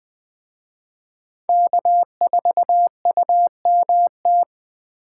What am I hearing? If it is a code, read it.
Morse code, heard as a keyed tone, about 21 wpm, 700 Hz. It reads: K4UMT
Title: Speed Racing - USA 1x3 Call Signs 17wpm